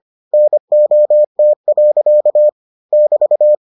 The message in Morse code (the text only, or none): NOT. <BT>